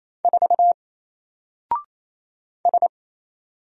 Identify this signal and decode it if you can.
Morse code, heard as a keyed tone, about 28 wpm, 700 Hz. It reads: 4 S
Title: Single Character (Letter:Number:Punctuation:Prosign) Alt 28wpm